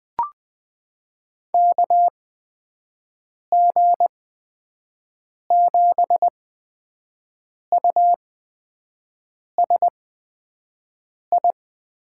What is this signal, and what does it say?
Morse code, heard as a keyed tone, about 20 wpm, spaced as for 5 wpm, 700 Hz. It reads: KG7USI
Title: USA Call Signs - 8x Character Spacing - 20wpm